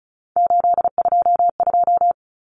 Morse code, heard as a keyed tone, about 34 wpm, 700 Hz. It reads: 822
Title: Numbers - 3-Digit 35wpm